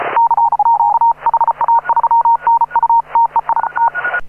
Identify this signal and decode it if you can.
Morse code, heard as a keyed tone, about 34 wpm, 930 Hz. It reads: CQ HA3NU TEI T
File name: AUDIO2018 / HA9A CW